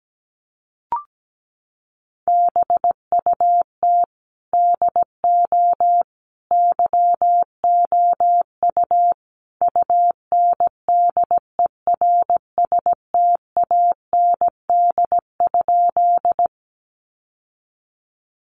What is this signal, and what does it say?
Morse code, heard as a keyed tone, about 17 wpm, 700 Hz. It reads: BUT DO YOU UNDERSTAND?
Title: Sentences from Top 600 Words 17wpm